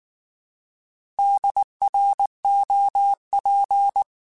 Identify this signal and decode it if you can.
Morse code, heard as a keyed tone, about 19 wpm, 800 Hz. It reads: DROP
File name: Alphabetical / Alphabetical mp3s / Ds @18